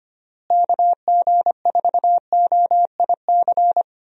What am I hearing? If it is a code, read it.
Morse code, heard as a keyed tone, about 25 wpm, 700 Hz. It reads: KG4OIC
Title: USA Call Signs - 6 Characters - Repeated 2x - 25wpm